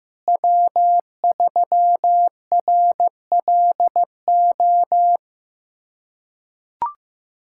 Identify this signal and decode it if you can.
Morse code, heard as a keyed tone, about 15 wpm, 700 Hz. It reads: W3RLO
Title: K1USN SST QSO Practice - 15wpm